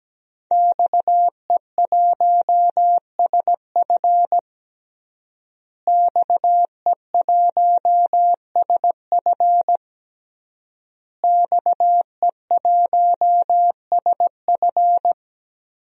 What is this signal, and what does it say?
Morse code, heard as a keyed tone, about 17 wpm, 700 Hz. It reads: XE1SF XE1SF XE1SF
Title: DXCC-100 Call Signs - 5 Characters - Repeated 3x - 17wpm